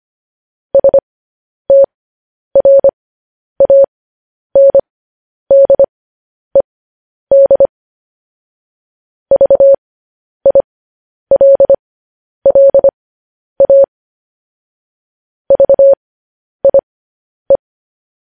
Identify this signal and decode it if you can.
Morse code, heard as a keyed tone, about 25 wpm, spaced as for 10 wpm, 550 Hz. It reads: STRANDED VILLA VIE
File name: podcast / News headlines-1x-25x10